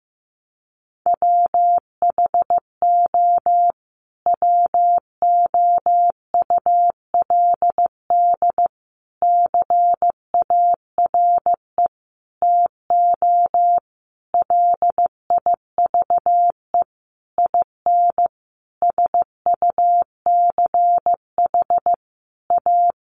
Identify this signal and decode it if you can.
Morse code, heard as a keyed tone, about 15 wpm, 700 Hz. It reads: WHO WOULD CARE TO LIVE IN SUCH A
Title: Sentences from Top 1500 Words 15wpm